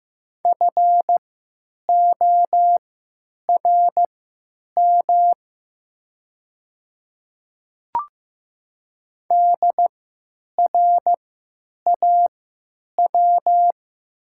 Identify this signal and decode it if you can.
Morse code, heard as a keyed tone, about 15 wpm, spaced as for 9 wpm, 700 Hz. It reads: FORM DRAW
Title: 3x Character Spacing - Top 500 Words - 15wpm